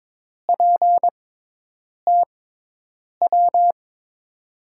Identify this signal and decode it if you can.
Morse code, heard as a keyed tone, about 22 wpm, spaced as for 8 wpm, 700 Hz. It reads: PTW
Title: USA Call Signs - 6 Characters - 6x Character Spacing - 22wpm